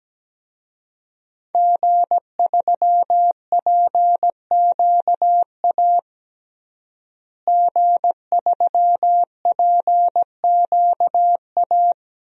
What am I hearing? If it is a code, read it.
Morse code, heard as a keyed tone, about 17 wpm, 700 Hz. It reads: G3PQA G3PQA
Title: DXCC-100 Call Signs - 5 Characters - Repeated 2x - 17wpm